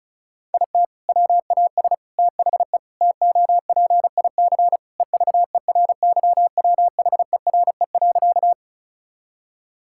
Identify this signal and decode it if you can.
Morse code, heard as a keyed tone, about 34 wpm, 700 Hz. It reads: IT WAS THE TOPIC EVERYWHERE.
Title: Sentences from Top 1600 Words 35wpm